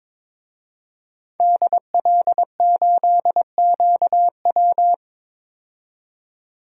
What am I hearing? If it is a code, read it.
Morse code, heard as a keyed tone, about 22 wpm, 700 Hz. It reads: DL8QW
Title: DXCC-100 Call Signs - 5 Characters - 22wpm